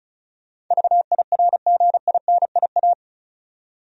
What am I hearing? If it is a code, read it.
Morse code, heard as a keyed tone, about 36 wpm, 700 Hz. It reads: VIRGINIA
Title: USA Common Names - 35wpm